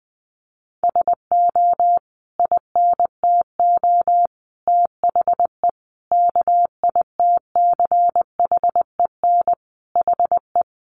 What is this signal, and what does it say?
Morse code, heard as a keyed tone, about 20 wpm, 700 Hz. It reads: SO INTO THE KITCHEN HE
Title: Sentences from Top 1400 Words 20wpm